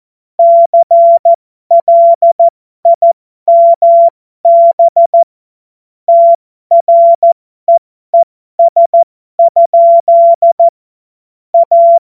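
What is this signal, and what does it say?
Morse code, heard as a keyed tone, about 14 wpm, spaced as for 12 wpm, 680 Hz. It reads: CLIMB TREES? A
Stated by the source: Morse code joke practice.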